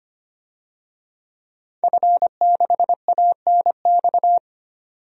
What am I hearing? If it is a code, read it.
Morse code, heard as a keyed tone, about 25 wpm, 700 Hz. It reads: F6ANX
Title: DXCC-100 Call Signs - 5 Characters - 25wpm